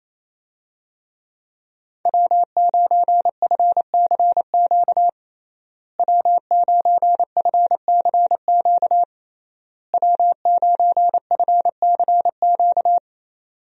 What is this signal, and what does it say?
Morse code, heard as a keyed tone, about 28 wpm, 700 Hz. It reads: W9FCQ W9FCQ W9FCQ
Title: USA Call Signs - 5 Characters - Repeated 3x - 28wpm